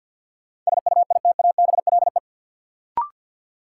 Morse code, heard as a keyed tone, about 49 wpm, 700 Hz. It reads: SUITABLE
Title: Sentences from Top 1400 Words - New Word Drill - 50wpm